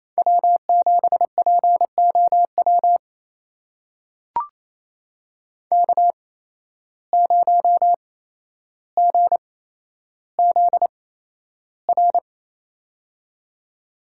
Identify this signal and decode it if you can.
Morse code, heard as a keyed tone, about 28 wpm, spaced as for 12 wpm, 700 Hz. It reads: W7POW K0GZR
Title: USA Call Signs - 5 Characters - 8x Character Spacing - 28wpm